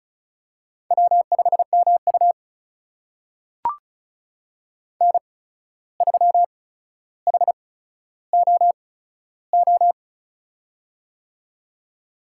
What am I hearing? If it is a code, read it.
Morse code, heard as a keyed tone, about 36 wpm, spaced as for 14 wpm, 700 Hz. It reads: W5MU N3HOO
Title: USA Call Signs - 8x Character Spacing - 35wpm